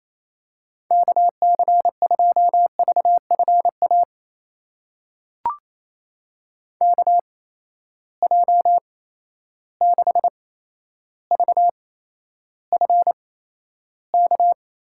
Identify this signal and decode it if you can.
Morse code, heard as a keyed tone, about 28 wpm, spaced as for 12 wpm, 700 Hz. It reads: KC2VFA KJ6VFK
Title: USA Call Signs - 6 Characters - 8x Character Spacing - 28wpm